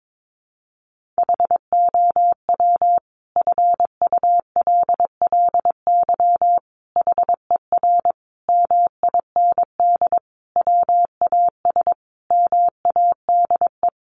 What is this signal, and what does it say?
Morse code, heard as a keyed tone, about 22 wpm, 700 Hz. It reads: HOW FULLY HER MIND WAS MADE
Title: Sentences from Top 1300 Words 22wpm